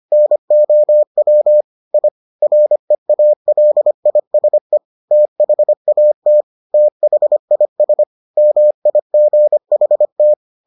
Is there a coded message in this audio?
Yes: Morse code from 0.1 to 10.3 s, heard as a keyed tone, about 25 wpm, 600 Hz. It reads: NOW I REALISE THAT THIS MIGHT